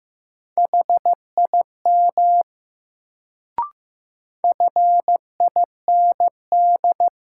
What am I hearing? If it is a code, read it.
Morse code, heard as a keyed tone, about 15 wpm, 700 Hz. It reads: HIM FIND
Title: Sets of 2 Words - New Word Drill - 15wpm